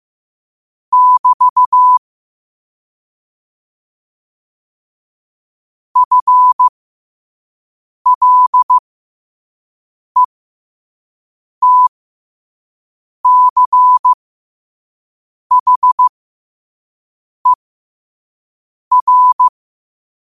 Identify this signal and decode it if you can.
Morse code, heard as a keyed tone, about 15 wpm, spaced as for 5 wpm, 1000 Hz. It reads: <BT> FLETCHER
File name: qotdfiles / 5wpm